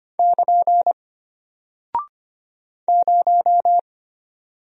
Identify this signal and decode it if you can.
Morse code, heard as a keyed tone, about 25 wpm, 700 Hz. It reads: <KN> 0
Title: Single Character (Letter:Number:Punctuation:Prosign) Alt 25wpm